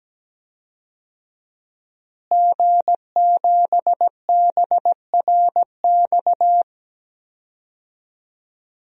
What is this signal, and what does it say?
Morse code, heard as a keyed tone, about 17 wpm, 700 Hz. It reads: G7BRX
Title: DXCC-100 Call Signs - 5 Characters - Repeated 2x - 17wpm